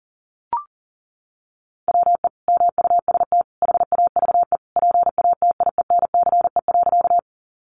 Morse code, heard as a keyed tone, about 40 wpm, 700 Hz. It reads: WE MUST HAVE PATIENCE.
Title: Sentences from Top 1300 Words 40wpm